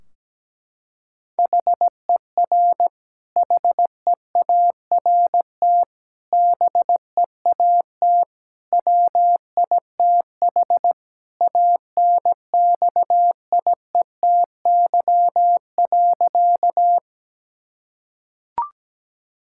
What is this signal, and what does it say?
Morse code, heard as a keyed tone, about 17 wpm, 700 Hz. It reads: HER HEART BEAT WITH ANXIETY.
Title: Sentences from Top 1600 Words 17wpm